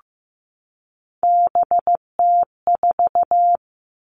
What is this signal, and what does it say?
Morse code, heard as a keyed tone, about 15 wpm, 700 Hz. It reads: BT4
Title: All Cut Numbers - 3-Digit 15wpm